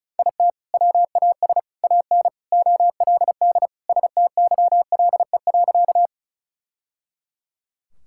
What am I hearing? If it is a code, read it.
Morse code, heard as a keyed tone, about 36 wpm, 700 Hz. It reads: IT WAS AN OLD STYLE.
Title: Sentences from Top 1000 Words - Review Part 5 - 35wpm